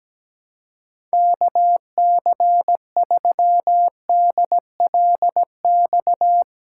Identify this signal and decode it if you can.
Morse code, heard as a keyed tone, about 17 wpm, 700 Hz. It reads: KC3DLX
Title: USA Call Signs - 6 Characters - 4x Character Spacing - 17wpm